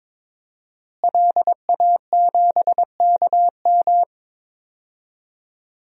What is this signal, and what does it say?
Morse code, heard as a keyed tone, about 22 wpm, 700 Hz. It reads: LA7KM
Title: DXCC-100 Call Signs - 5 Characters - Repeated 2x - 22wpm